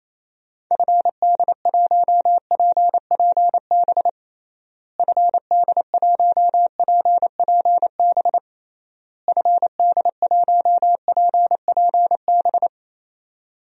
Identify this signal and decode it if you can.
Morse code, heard as a keyed tone, about 28 wpm, 700 Hz. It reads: FD1PPB FD1PPB FD1PPB
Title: DXCC-100 Call Signs - 6 Characters - Repeated 3x - 28wpm